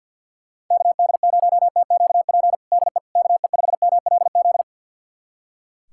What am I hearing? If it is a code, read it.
Morse code, heard as a keyed tone, about 45 wpm, 700 Hz. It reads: KD0TQP DE KE5MLZ
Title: USA Callsign DE Callsign - 6 Characters - 50wpm